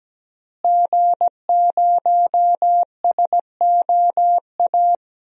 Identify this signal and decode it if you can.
Morse code, heard as a keyed tone, about 17 wpm, 700 Hz. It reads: G0SOA
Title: DXCC-100 Call Signs - 5 Characters - Repeated 3x - 17wpm